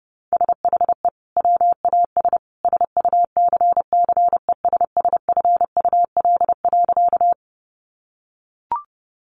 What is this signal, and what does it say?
Morse code, heard as a keyed tone, about 30 wpm, 700 Hz. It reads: SHE WAS SUCCESSFUL.
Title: Sentences from Top 900 Words 30wpm